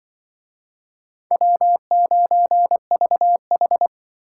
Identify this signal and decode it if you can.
Morse code, heard as a keyed tone, about 24 wpm, 700 Hz. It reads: W9VH
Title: Speed Racing - USA Call Signs 20wpm